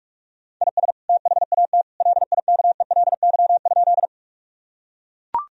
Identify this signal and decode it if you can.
Morse code, heard as a keyed tone, about 41 wpm, 700 Hz. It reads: IS THAT LIKELY?
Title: Sentences from Top 1000 Words - Review Part 4 - 45wpm